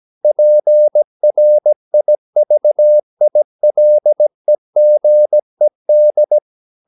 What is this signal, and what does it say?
Morse code, heard as a keyed tone, about 17 wpm, 600 Hz. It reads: PRIVILEGED
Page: Romanen Never af Ken Follet på engelsk.